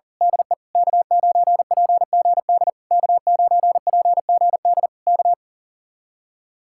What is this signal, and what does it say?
Morse code, heard as a keyed tone, about 36 wpm, 700 Hz. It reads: DE K9PGD K9PGD K
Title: Speed-Racing Common QSO Phrases - 40wpm